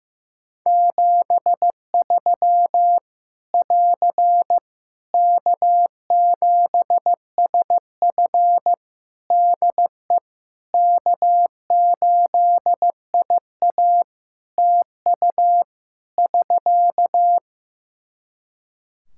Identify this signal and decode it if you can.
Morse code, heard as a keyed tone, about 15 wpm, 700 Hz. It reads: 73 <AR> K7SF DE K8IA TU <SK>
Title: Practice QSO - Exchange 4 - Ending - 15wpm